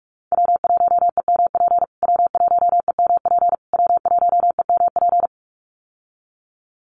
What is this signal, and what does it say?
Morse code, heard as a keyed tone, about 44 wpm, 700 Hz. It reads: W1EMP W1EMP W1EMP
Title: USA Call Signs Repeated 3x 45wpm